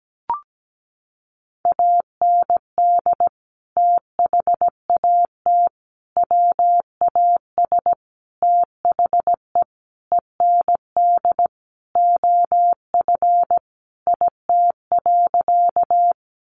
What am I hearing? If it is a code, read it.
Morse code, heard as a keyed tone, about 17 wpm, 700 Hz. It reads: AND THAT WAS THE END OF IT.
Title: Sentences from Top 200 Words 17wpm